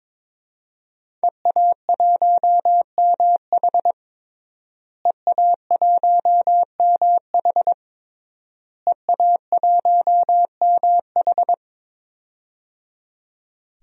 Morse code, heard as a keyed tone, about 21 wpm, 700 Hz. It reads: EA1MH EA1MH EA1MH
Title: DXCC-100 Call Signs - 5 Characters - Repeated 3x - 22wpm